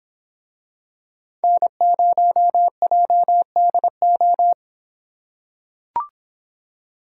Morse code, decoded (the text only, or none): N0JDO